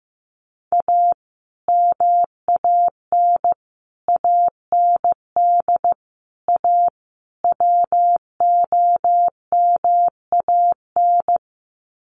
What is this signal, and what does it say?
Morse code, heard as a keyed tone, about 15 wpm, 700 Hz. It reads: A MAN AND A WOMAN
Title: Sets of 5 Words 15wpm